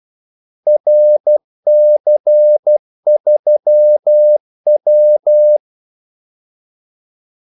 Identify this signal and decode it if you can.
Morse code, heard as a keyed tone, about 12 wpm, 600 Hz. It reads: RC3W
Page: 2 ekstra ordmellemrum mellem kaldesignalerne